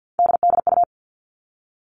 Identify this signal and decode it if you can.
Morse code, heard as a keyed tone, about 48 wpm, 700 Hz. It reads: DDU
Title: All Cut Numbers - 3-Digit 50wpm